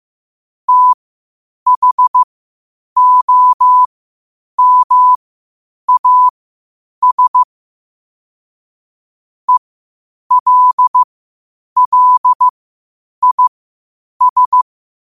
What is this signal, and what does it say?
Morse code, heard as a keyed tone, about 15 wpm, spaced as for 8 wpm, 1000 Hz. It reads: THOMAS ELLIS
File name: 8wpm